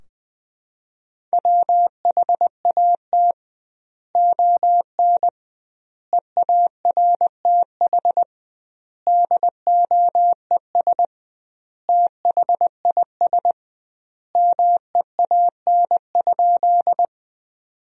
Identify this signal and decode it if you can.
Morse code, heard as a keyed tone, about 20 wpm, 700 Hz. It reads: WHAT ON EARTH DOES THIS MEAN?
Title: 2x Word-Spacing - Sentences from Top 200 Words - 20wpm